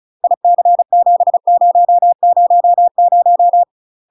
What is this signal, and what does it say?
Morse code, heard as a keyed tone, about 36 wpm, 700 Hz. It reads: IC7000
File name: UK MASTER QSOs TRAINING / 35WPM / UK MASTER QSOs 0038